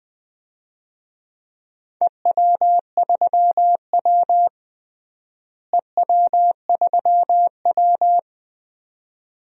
Morse code, heard as a keyed tone, about 20 wpm, 700 Hz. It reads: EW3W EW3W
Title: DXCC-100 Call Signs - 4 Characters - Repeated 2x - 20wpm